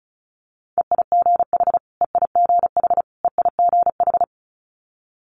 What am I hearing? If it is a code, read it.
Morse code, heard as a keyed tone, about 36 wpm, 700 Hz. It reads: EIGH EIGH EIGH
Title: Multi-letter Phonemes Repeated 3x 35wpm